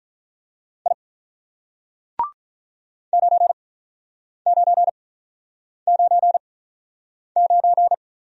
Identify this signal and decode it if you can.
Morse code, heard as a keyed tone, about 45 wpm, spaced as for 9 wpm, 700 Hz. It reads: I 9999